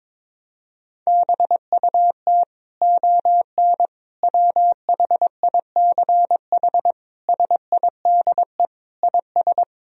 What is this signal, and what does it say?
Morse code, heard as a keyed tone, about 22 wpm, 700 Hz. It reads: BUT ON WHICH SIDE IS
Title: Sentences from Top 1000 Words - Review Part 1 - 22wpm